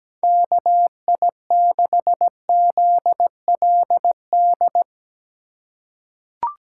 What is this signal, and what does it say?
Morse code, heard as a keyed tone, about 17 wpm, 700 Hz. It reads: KI6ZLD
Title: USA Call Signs - 3x Character Spacing - 17wpm